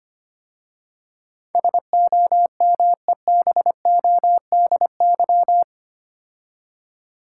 Morse code, heard as a keyed tone, about 25 wpm, 700 Hz. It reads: SOMEBODY
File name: 8 Letter Words - First 3 Letters Given - 25wpm